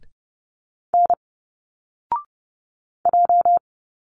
Morse code, heard as a keyed tone, about 30 wpm, 700 Hz. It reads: N J
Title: Single Letters 30wpm